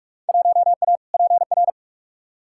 Morse code, heard as a keyed tone, about 46 wpm, 700 Hz. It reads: 1A PR
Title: Field Day Exchanges Encore - 45wpm